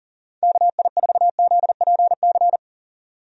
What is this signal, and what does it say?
Morse code, heard as a keyed tone, about 40 wpm, 700 Hz. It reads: KI4ZPC
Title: USA Call Signs - 6 Characters - 40wpm